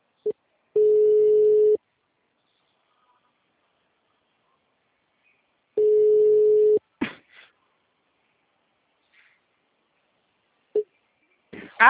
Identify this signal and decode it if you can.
Ringback tone (Europe, 425 Hz), with speech.
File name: out- -2004- - - .24112